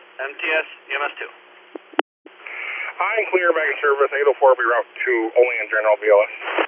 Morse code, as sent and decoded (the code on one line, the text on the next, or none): none